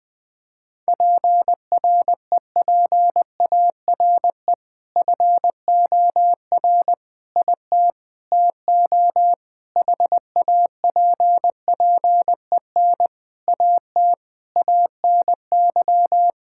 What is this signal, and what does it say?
Morse code, heard as a keyed tone, about 20 wpm, 700 Hz. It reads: PREPARE FOR IT TO HAPPEN AT ANY
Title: Sentences from Top 1000 Words - Review Part 6 - 20wpm